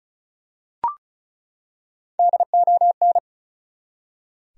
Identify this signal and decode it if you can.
Morse code, heard as a keyed tone, about 36 wpm, 700 Hz. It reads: DON
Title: USA Common Names - 35wpm